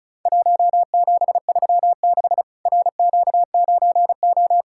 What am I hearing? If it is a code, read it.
Morse code, heard as a keyed tone, about 36 wpm, 700 Hz. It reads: 1736 RQ9O
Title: Worked All Europe DX Contest - QTC Exchange - 35wpm